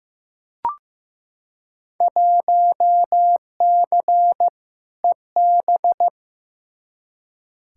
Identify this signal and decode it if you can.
Morse code, heard as a keyed tone, about 15 wpm, 700 Hz. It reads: 1C EB
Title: Field Day Exchanges Encore - 15wpm